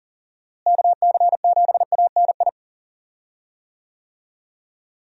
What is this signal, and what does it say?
Morse code, heard as a keyed tone, about 39 wpm, 700 Hz. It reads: KC7ANI
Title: USA Call Signs - 6 Characters - Repeated 3x - 40wpm